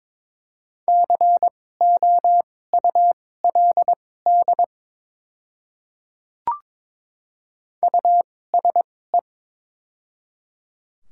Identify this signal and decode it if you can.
Morse code, heard as a keyed tone, about 23 wpm, spaced as for 16 wpm, 700 Hz. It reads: COULD USE
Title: 2x Character Spacing - Top 100 Words - 22wpm